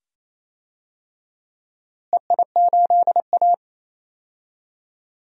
Morse code, heard as a keyed tone, about 28 wpm, 700 Hz. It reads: EI8A
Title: DXCC-100 Call Signs - 4 Characters - Repeated 2x - 28wpm